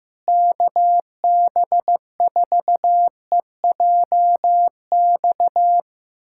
Morse code, heard as a keyed tone, about 15 wpm, 700 Hz. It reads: KB4EJX
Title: USA Callsign DE Callsign - 6 Characters - 15wpm